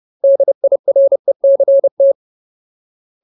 Morse code, heard as a keyed tone, about 30 wpm, 550 Hz. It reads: DIRECT
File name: Ultimate Word List / 30WPM Ultimate